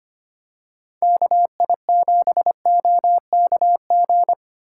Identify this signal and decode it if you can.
Morse code, heard as a keyed tone, about 25 wpm, 700 Hz. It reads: KI7OKG
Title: USA Call Signs - 2x Character Spacing - 25wpm